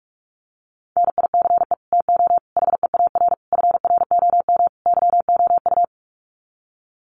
Morse code, heard as a keyed tone, about 44 wpm, 700 Hz. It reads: NICE TO HEAR FROM YOU
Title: Common QSO Phrases 45wpm